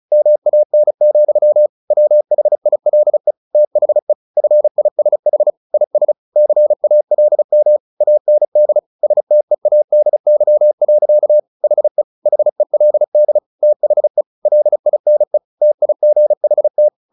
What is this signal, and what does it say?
Morse code, heard as a keyed tone, about 34 wpm, 600 Hz. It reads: MAN, WHILE THE FISH IS CALM AND STEADY. HE HELD THE LINE TIGHT
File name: the old man and the sea 2 35wpm